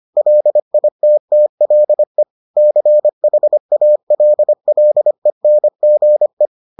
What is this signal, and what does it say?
Morse code, heard as a keyed tone, about 25 wpm, 600 Hz. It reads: LITTLE CHALLENGE